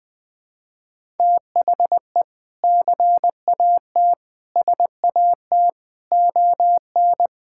Morse code, heard as a keyed tone, about 21 wpm, 700 Hz. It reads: THE CAT SAT ON
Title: Sentences with 2-Word Phrases - 20wpm